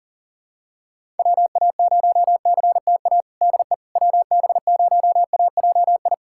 Morse code, heard as a keyed tone, about 40 wpm, 700 Hz. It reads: WA0CTA DE WB0AJI
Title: USA Callsign DE Callsign - 6 Characters - 40wpm